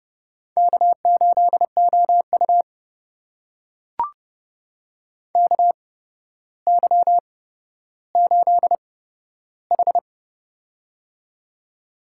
Morse code, heard as a keyed tone, about 31 wpm, spaced as for 13 wpm, 700 Hz. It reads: K8OU KY8H
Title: USA Call Signs - 4 Characters - 8x Character Spacing - 30wpm